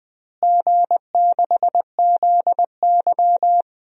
Morse code, heard as a keyed tone, about 20 wpm, 700 Hz. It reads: G6ZY
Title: DXCC-100 Call Signs - 4 Characters - Repeated 2x - 20wpm